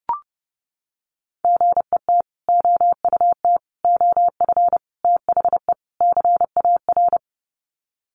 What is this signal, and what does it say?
Morse code, heard as a keyed tone, about 30 wpm, 700 Hz. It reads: GET OUT OF THE CAR
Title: Sets of 5 Words Encore2 30wpm